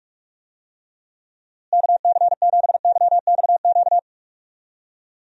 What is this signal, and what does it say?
Morse code, heard as a keyed tone, about 44 wpm, 700 Hz. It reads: KC7YXQ
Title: USA Call Signs - 6 Characters - Repeated 3x - 45wpm